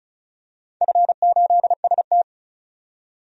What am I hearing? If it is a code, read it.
Morse code, heard as a keyed tone, about 36 wpm, 700 Hz. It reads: F8ST
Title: DXCC-100 Call Signs - 4 Characters - 35wpm